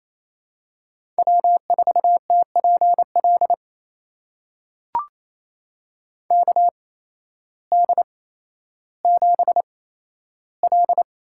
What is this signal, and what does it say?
Morse code, heard as a keyed tone, about 28 wpm, spaced as for 13 wpm, 700 Hz. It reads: W4TPL KD7L
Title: USA Call Signs - 8x Character Spacing - 28wpm